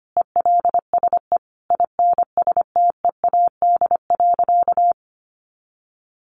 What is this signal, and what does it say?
Morse code, heard as a keyed tone, about 25 wpm, 700 Hz. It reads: ELSE INSTEAD.
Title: Sentences from Top 800 Words 25wpm